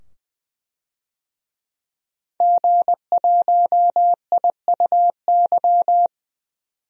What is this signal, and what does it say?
Morse code, heard as a keyed tone, about 20 wpm, 700 Hz. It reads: G1IUY
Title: DXCC-100 Call Signs - 5 Characters - 20wpm